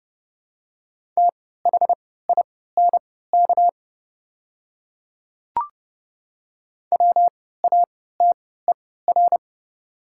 Morse code, heard as a keyed tone, about 29 wpm, spaced as for 17 wpm, 700 Hz. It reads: THINK WATER
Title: 3x Character Spacing - Top 200 Words - 30wpm